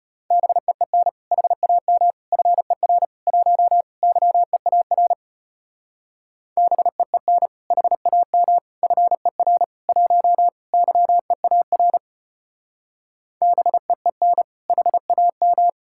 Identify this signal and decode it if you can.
Morse code, heard as a keyed tone, about 31 wpm, 700 Hz. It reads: BEEN HAM FER 1 YEAR BEEN HAM FER 1 YEAR BEEN HAM